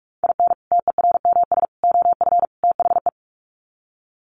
Morse code, heard as a keyed tone, about 44 wpm, 700 Hz. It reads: IN TERMS OF THE
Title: Sets of 4 Words 45wpm